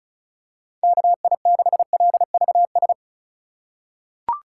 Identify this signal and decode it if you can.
Morse code, heard as a keyed tone, about 36 wpm, 700 Hz. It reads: KI6LVS
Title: USA Call Signs - 6 Characters - 2x Character Spacing - 35wpm